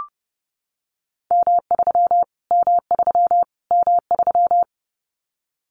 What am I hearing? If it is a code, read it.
Morse code, heard as a keyed tone, about 30 wpm, 700 Hz. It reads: M3 M3 M3
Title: Callsign Prefixes for Top 25 DXCC - Repeated 3x - 30wpm